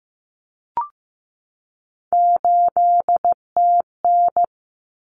Morse code, heard as a keyed tone, about 15 wpm, 700 Hz. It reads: 8TN
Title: Common Cut Numbers - 3-Digit 15wpm